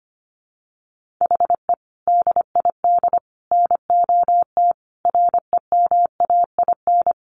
Morse code, heard as a keyed tone, about 25 wpm, 700 Hz. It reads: HE DID NOT REMAIN
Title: Sentences from Top 1500 Words 25wpm